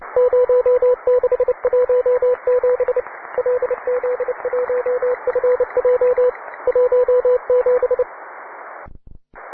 Morse code, heard as a keyed tone, about 29 wpm, 500 Hz, with radio noise. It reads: 0617 LZ1FJ 17